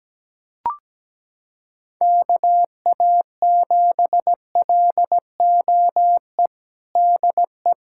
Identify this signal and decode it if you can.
Morse code, heard as a keyed tone, about 17 wpm, 700 Hz. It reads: KA7LOE DE
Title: USA Callsign DE Callsign - 6 Characters - 17wpm